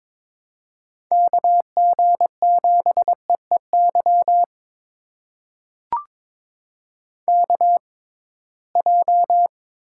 Morse code, heard as a keyed tone, about 22 wpm, spaced as for 17 wpm, 700 Hz. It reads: KG7EEY KJ
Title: USA Call Signs - 6 Characters - 6x Character Spacing - 22wpm